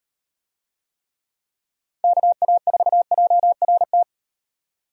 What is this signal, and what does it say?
Morse code, heard as a keyed tone, about 36 wpm, 700 Hz. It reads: KA4JRT